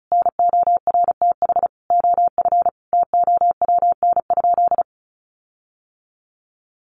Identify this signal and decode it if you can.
Morse code, heard as a keyed tone, about 34 wpm, 700 Hz. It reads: NORTH OF TOWN?
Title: Sentences from Top 1300 Words 35wpm